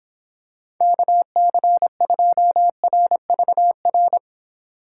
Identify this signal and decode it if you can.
Morse code, heard as a keyed tone, about 26 wpm, 700 Hz. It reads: KC2RVR